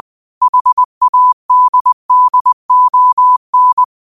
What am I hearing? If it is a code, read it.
Morse code, heard as a keyed tone, about 20 wpm, 1000 Hz. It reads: HADDON
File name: qotdfiles / 20wpm